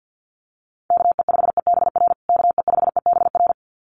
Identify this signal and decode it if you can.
Morse code, heard as a keyed tone, about 50 wpm, 700 Hz. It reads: KE5EBR KE5EBR
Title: USA Call Signs Repeated 2x 50wpm